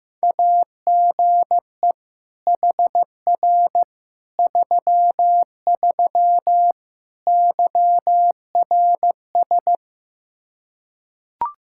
Morse code, heard as a keyed tone, about 15 wpm, 700 Hz. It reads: AGE HR 33 YRS
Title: Practice QSO - Exchange 3 - More Info - 15wpm